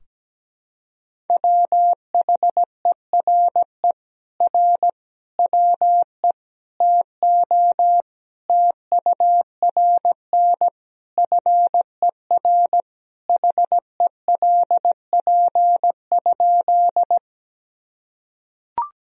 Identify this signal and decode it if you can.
Morse code, heard as a keyed tone, about 17 wpm, 700 Hz. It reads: WHERE R WE TO TURN FER HELP?
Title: Sentences from Top 200 Words and Abbreviations 17wpm